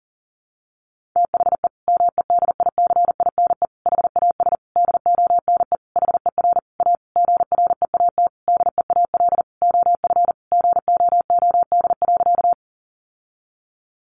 Morse code, heard as a keyed tone, about 40 wpm, 700 Hz. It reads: THE MEDICINE HAS DONE HER A GREAT DEAL OF GOOD.
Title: Sentences from Top 900 Words 40wpm